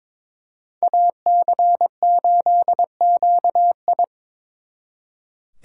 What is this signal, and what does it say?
Morse code, heard as a keyed tone, about 26 wpm, 700 Hz. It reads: AC8QI
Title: Speed Racing - USA 2x2 Call Signs 22wpm